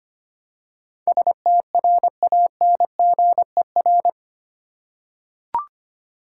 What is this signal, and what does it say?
Morse code, heard as a keyed tone, about 25 wpm, 700 Hz. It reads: STRANGER
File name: Eight Letter Words - 25wpm